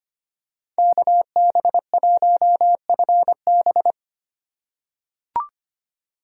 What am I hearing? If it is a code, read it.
Morse code, heard as a keyed tone, about 25 wpm, 700 Hz. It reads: KB1FB
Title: USA Call Signs - 5 Characters - 8x Character Spacing - 25wpm